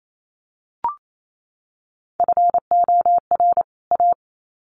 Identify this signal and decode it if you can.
Morse code, heard as a keyed tone, about 28 wpm, 700 Hz. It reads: FOR A
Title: Sets of 2 Words 28wpm